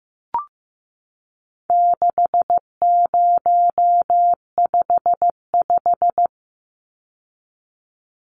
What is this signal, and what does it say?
Morse code, heard as a keyed tone, about 15 wpm, 700 Hz. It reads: 6055
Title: Numbers - 4-Digit 15wpm